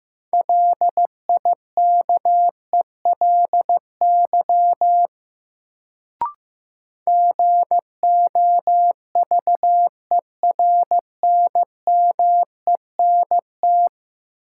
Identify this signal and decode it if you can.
Morse code, heard as a keyed tone, about 15 wpm, 700 Hz. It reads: LIKELY GOVERNMENT
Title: Sentences from Top 600 Words - New Word Drill - 15wpm